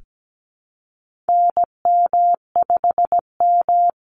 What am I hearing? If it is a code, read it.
Morse code, heard as a keyed tone, about 17 wpm, 700 Hz. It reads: NM5M
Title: Call Signs Repeated 2x - 17wpm